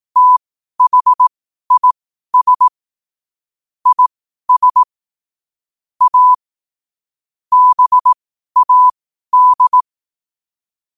Morse code, heard as a keyed tone, about 18 wpm, spaced as for 12 wpm, 1000 Hz. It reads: THIS IS A BAD
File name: qotdfiles / 12wpm